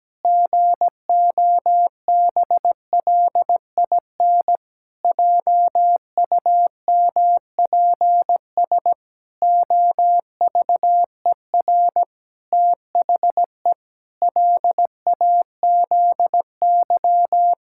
Morse code, heard as a keyed tone, about 17 wpm, 700 Hz. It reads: GOBLIN JUMPS OVER THE LAZY
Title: Pangram Challenge v2 - 17wpm